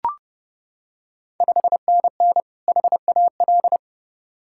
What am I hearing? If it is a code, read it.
Morse code, heard as a keyed tone, about 32 wpm, 700 Hz. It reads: 5NN HAL
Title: Ontario QSO Party Exchange 30wpm